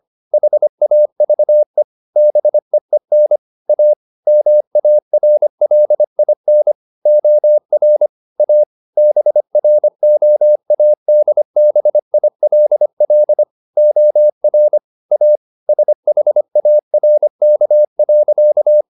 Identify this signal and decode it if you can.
Morse code, heard as a keyed tone, about 25 wpm, 600 Hz. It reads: HAVE BEEN A MARLIN OR A BROADBILL OR A SHARK.
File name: the old man and the sea part 1 25wpm